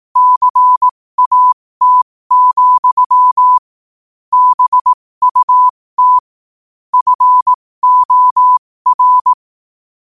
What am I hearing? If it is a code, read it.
Morse code, heard as a keyed tone, about 18 wpm, spaced as for 15 wpm, 1000 Hz. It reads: CAT, BUT FOR
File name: qotdfiles / 15wpm